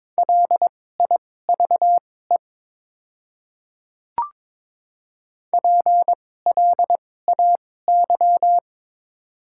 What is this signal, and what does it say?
Morse code, heard as a keyed tone, about 22 wpm, spaced as for 16 wpm, 700 Hz. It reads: LIVE PLAY
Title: 2x Character Spacing - Top 300 Words - 22wpm